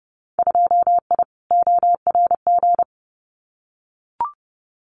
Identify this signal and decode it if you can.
Morse code, heard as a keyed tone, about 30 wpm, 700 Hz. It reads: 2I ORG
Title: Winter Field Day Exchanges Encore 30wpm